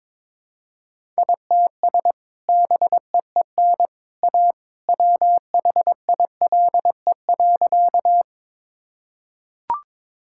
Morse code, heard as a keyed tone, about 22 wpm, 700 Hz. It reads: ITS BEEN A WHILE.
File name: practice / 22wpm / Sentences with 2-Word Phrases - 22wpm